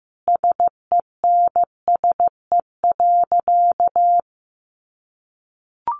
Morse code, heard as a keyed tone, about 15 wpm, 700 Hz. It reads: SENSE.
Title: Sentences from Top 1500 Words 15wpm